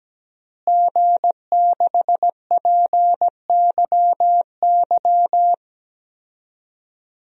Morse code, heard as a keyed tone, about 17 wpm, 700 Hz. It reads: G6PYY
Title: DXCC-100 Call Signs - 5 Characters - Repeated 3x - 17wpm